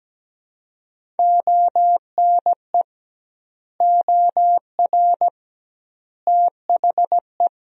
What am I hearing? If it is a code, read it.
Morse code, heard as a keyed tone, about 17 wpm, 700 Hz. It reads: ONE OR THE
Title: 2x Word-Spacing - Sentences from Top 400 Words - 17wpm